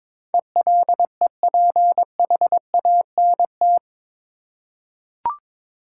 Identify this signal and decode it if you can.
Morse code, heard as a keyed tone, about 22 wpm, 700 Hz. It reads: ELEPHANT
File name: practice / 22wpm / Eight Letter Words - 22wpm